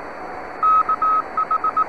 Morse code, heard as a keyed tone, about 18 wpm, 1280 Hz. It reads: KH
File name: sat / FO-29 15Mar2010